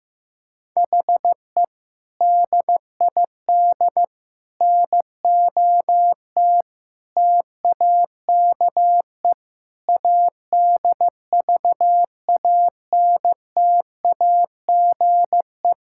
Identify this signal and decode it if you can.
Morse code, heard as a keyed tone, about 16 wpm, 700 Hz. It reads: HE DID NOT TAKE ADVANTAGE
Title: Sentences from Top 1000 Words - Review Part 6 - 15wpm